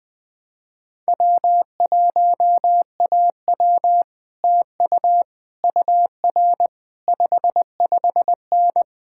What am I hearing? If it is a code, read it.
Morse code, heard as a keyed tone, about 20 wpm, 700 Hz. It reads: W1AW TU UR 55N
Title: Speed-Racing Common QSO Phrases - 20wpm